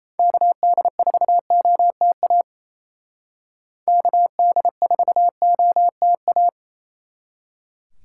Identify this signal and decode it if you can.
Morse code, heard as a keyed tone, about 36 wpm, 700 Hz. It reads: KD4OTA KD4OTA